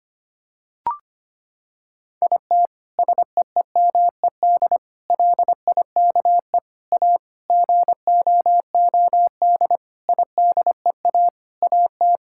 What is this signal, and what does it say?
Morse code, heard as a keyed tone, about 25 wpm, 700 Hz. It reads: IT SEEMED LIKE A GOOD IDEA AT
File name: practice / 25wpm / Sentences with 9-Word Phrases - 25wpm